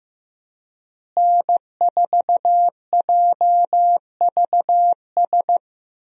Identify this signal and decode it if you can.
Morse code, heard as a keyed tone, about 15 wpm, 700 Hz. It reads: N4JVS
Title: USA Call Signs - 5 Characters - Repeated 3x - 15wpm